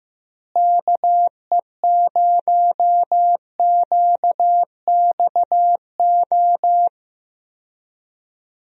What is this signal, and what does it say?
Morse code, heard as a keyed tone, about 15 wpm, 700 Hz. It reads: KE0QXO
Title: USA Call Signs - 6 Characters - Repeated 3x - 15wpm